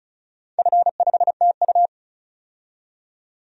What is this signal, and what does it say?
Morse code, heard as a keyed tone, about 36 wpm, 690 Hz. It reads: F5TU
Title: DXCC-100 Call Signs - 4 Characters - 35wpm